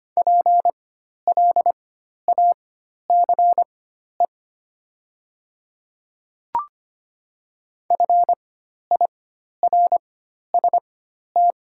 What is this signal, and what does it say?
Morse code, heard as a keyed tone, about 25 wpm, spaced as for 12 wpm, 700 Hz. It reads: PLACE FIRST
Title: 4x Character Spacing - Top 200 Words - 25wpm